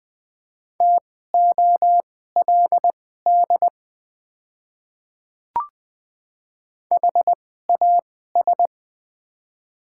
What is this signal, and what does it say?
Morse code, heard as a keyed tone, about 20 wpm, spaced as for 14 wpm, 690 Hz. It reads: TOLD HAS
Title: 2x Character Spacing - Top 500 Words - 20wpm